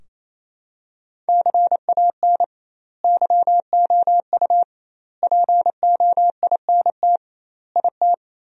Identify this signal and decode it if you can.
Morse code, heard as a keyed tone, about 28 wpm, 700 Hz. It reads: CAN YOU POINT IT
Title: 2x Word-Spacing - Sentences from Top 200 Words - 28wpm